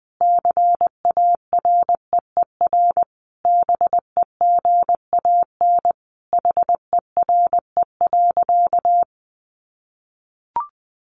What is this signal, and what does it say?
Morse code, heard as a keyed tone, about 20 wpm, 700 Hz. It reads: CAREER BEGAN HERE.
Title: Sentences from Top 1100 Words 20wpm